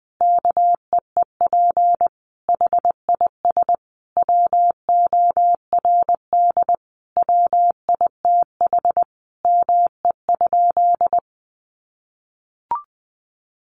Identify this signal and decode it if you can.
Morse code, heard as a keyed tone, about 19 wpm, 700 Hz. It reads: KEEP HIS WORD WITH ME?
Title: Sentences from Top 300 Words 20wpm